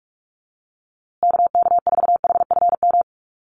Morse code, heard as a keyed tone, about 44 wpm, 700 Hz. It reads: KK4HFM
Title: USA Call Signs Repeated 3x 45wpm